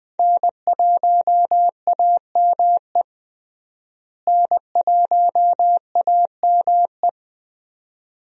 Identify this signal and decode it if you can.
Morse code, heard as a keyed tone, about 21 wpm, 700 Hz. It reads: N1AME N1AME
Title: USA Call Signs - 5 Characters - Repeated 2x - 20wpm